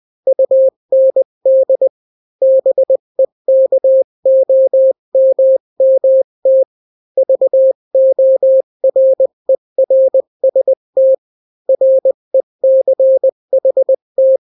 Morse code, heard as a keyed tone, about 20 wpm, 530 Hz. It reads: UND BEKOMMT VORERST RECHT
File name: daily / 18wpm